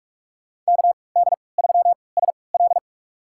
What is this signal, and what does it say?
Morse code, heard as a keyed tone, about 45 wpm, spaced as for 25 wpm, 700 Hz. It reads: KD3SL
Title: USA Call Signs - 5 Characters - 3x Character Spacing - 45wpm